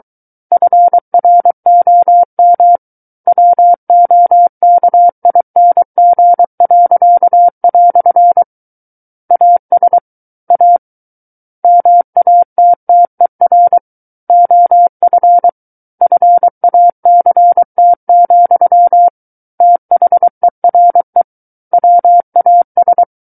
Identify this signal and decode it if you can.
Morse code, heard as a keyed tone, about 23 wpm, 700 Hz. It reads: FROM WOKING." AS A MATTER OF FACT, THERE WAS
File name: Book One / 14-Chapter 14 - 23 WPM